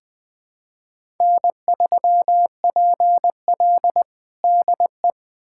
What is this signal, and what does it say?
Morse code, heard as a keyed tone, about 20 wpm, 700 Hz. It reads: N3PL DE
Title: USA Callsign DE Callsign - 4 Characters - 20wpm